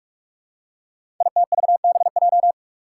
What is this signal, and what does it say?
Morse code, heard as a keyed tone, about 44 wpm, 700 Hz. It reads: ITVBJ
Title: Code Groups - 5 Letters - 45wpm